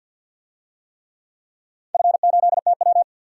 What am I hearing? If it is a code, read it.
Morse code, heard as a keyed tone, about 47 wpm, 700 Hz. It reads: F8TW
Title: DXCC-100 Call Signs - 4 Characters - 50wpm